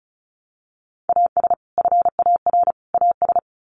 Morse code, heard as a keyed tone, about 34 wpm, 700 Hz. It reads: AS FAR AS
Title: Sets of 3 Words 35wpm